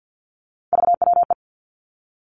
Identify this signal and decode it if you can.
Morse code, heard as a keyed tone, about 50 wpm, 700 Hz. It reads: VWE
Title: Three Challenging Letter-Number Combinations 50wpm